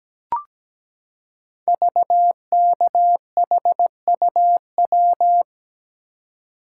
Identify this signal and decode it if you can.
Morse code, heard as a keyed tone, about 17 wpm, 700 Hz. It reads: VKHUW
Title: Code Groups - 5 Letters - 17wpm